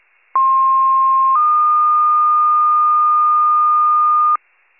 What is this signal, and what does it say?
Two-tone page, heard as a one-second tone, then a three-second tone, about 1060 then 1220 Hz, with noise. It is heard from 0.3 to 4.4 s.
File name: pager / 23 FIRE / wcalert